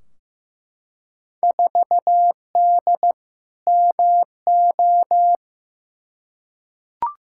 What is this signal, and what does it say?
Morse code, heard as a keyed tone, about 15 wpm, 700 Hz. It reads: 4D MO
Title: Field Day Exchanges Encore - 15wpm